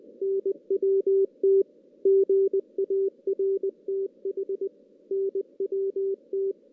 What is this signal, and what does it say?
Morse code, heard as a keyed tone, about 20 wpm, 390 Hz. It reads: NWT GARTH NWT
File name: 2016 JAN CW NAQP / 2016 Jan naqp k5ot - 0-6